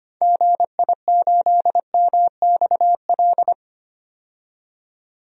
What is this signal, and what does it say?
Morse code, heard as a keyed tone, about 25 wpm, 700 Hz. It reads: GI8MXL
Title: DXCC-100 Call Signs - 6 Characters - 25wpm